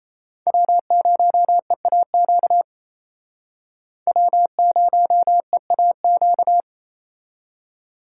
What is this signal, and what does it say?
Morse code, heard as a keyed tone, about 35 wpm, 700 Hz. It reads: W0EAQ W0EAQ